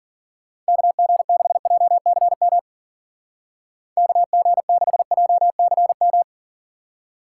Morse code, heard as a keyed tone, about 42 wpm, 700 Hz. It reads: KG6JCM KG6JCM
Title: Speed Racing - USA 2x3 Call Signs 40wpm